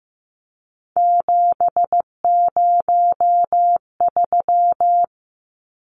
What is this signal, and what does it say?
Morse code, heard as a keyed tone, about 15 wpm, 700 Hz. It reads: 703
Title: Numbers - 3-Digit 15wpm